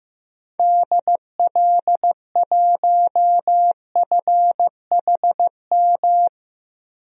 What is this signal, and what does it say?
Morse code, heard as a keyed tone, about 15 wpm, 700 Hz. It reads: DL1FHM
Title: DXCC-100 Call Signs - 6 Characters - 15wpm